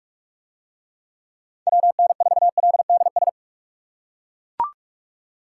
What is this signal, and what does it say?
Morse code, heard as a keyed tone, about 40 wpm, 700 Hz. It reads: WN4LDS